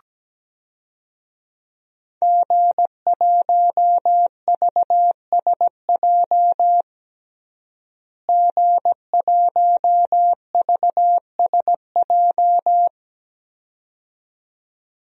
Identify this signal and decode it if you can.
Morse code, heard as a keyed tone, about 17 wpm, 700 Hz. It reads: G1VSJ G1VSJ
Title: DXCC-100 Call Signs - 5 Characters - Repeated 2x - 17wpm